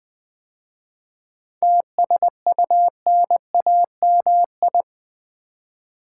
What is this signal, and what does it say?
Morse code, heard as a keyed tone, about 20 wpm, 700 Hz. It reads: TSUNAMI
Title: 7 Letter Words - First 3 Letters Given - 20wpm